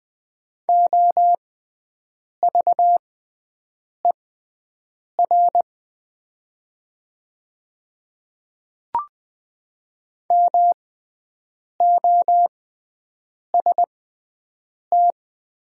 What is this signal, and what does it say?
Morse code, heard as a keyed tone, about 20 wpm, spaced as for 7 wpm, 700 Hz. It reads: OVER MOST
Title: 6x Character Spacing - Top 100 Words - 20wpm